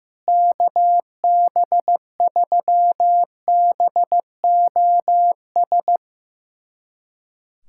Morse code, heard as a keyed tone, about 15 wpm, 700 Hz. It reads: KB3BOS
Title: USA Callsign DE Callsign 15wpm